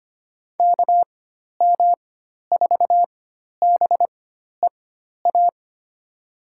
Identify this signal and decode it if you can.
Morse code, heard as a keyed tone, about 25 wpm, spaced as for 12 wpm, 700 Hz. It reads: KM4BEA
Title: USA Call Signs - 6 Characters - 4x Character Spacing - 25wpm